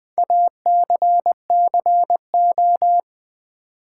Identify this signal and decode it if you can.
Morse code, heard as a keyed tone, about 20 wpm, 700 Hz. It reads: ACCO
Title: Build A Word - 4 Letters At A Time - 10-Letter Words - 20wpm